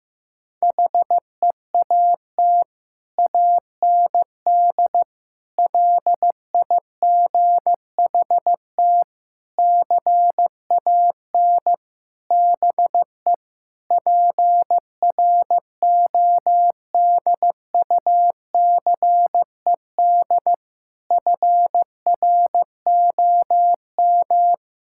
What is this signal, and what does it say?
Morse code, heard as a keyed tone, about 16 wpm, 700 Hz. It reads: HEAT AND LIGHT CAN BE PRODUCED FROM
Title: Sentences from Top 1000 Words - Review Part 6 - 15wpm